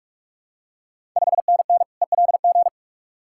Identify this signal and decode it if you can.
Morse code, heard as a keyed tone, about 45 wpm, 690 Hz. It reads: 5NN ELG
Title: Ontario QSO Party Exchange 45wpm